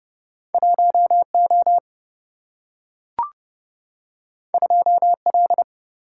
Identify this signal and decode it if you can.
Morse code, heard as a keyed tone, about 30 wpm, 700 Hz. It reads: 1 O 2 L
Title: Single-Letter Single-Number 30wpm